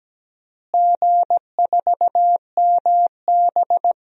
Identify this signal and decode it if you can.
Morse code, heard as a keyed tone, about 17 wpm, 700 Hz. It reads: G4MB
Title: DXCC-100 Call Signs - 4 Characters - Repeated 2x - 17wpm